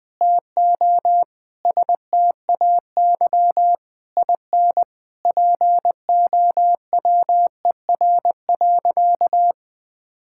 Morse code, heard as a keyed tone, about 20 wpm, 700 Hz. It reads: TO STAY IN POWER.
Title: Sentences from Top 1000 Words - Review Part 5 - 20wpm